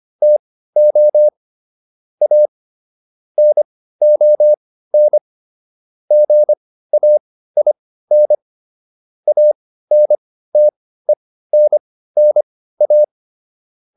Morse code, heard as a keyed tone, about 25 wpm, spaced as for 15 wpm, 600 Hz. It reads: TO A NON GAIN ANTENNA